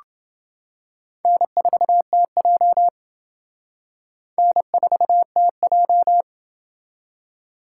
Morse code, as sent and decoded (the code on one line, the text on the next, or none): -. ....- - .--- / -. ....- - .---
N4TJ N4TJ